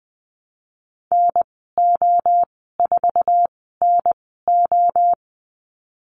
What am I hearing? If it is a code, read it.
Morse code, heard as a keyed tone, about 20 wpm, spaced as for 15 wpm, 700 Hz. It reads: NO4NO
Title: 048 Lesson 9g - Number 4 - Call Signs 20wpm